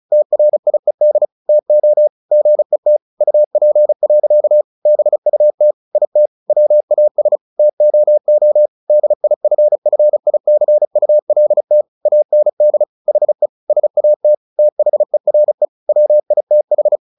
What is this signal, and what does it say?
Morse code, heard as a keyed tone, about 34 wpm, 600 Hz. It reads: TRIED TO GET UP. BUT IT WAS TOO DIFFICULT AND HE SAT THERE WITH
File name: the old man and the sea 2 35wpm